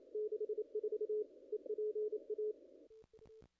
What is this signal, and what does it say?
Morse code, heard as a keyed tone, about 30 wpm, 410 Hz. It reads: 64 EPA
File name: contests / ss-cw-k5tr-op-n5rz -0100